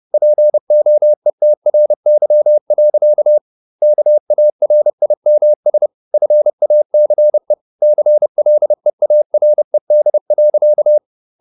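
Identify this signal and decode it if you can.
Morse code, heard as a keyed tone, about 30 wpm, 600 Hz. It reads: POETRY. KARIMS FACE CLEARED.
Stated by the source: Morse code at 30 wpm, the novel Never by Ken Follett in English, from CW med Gnister.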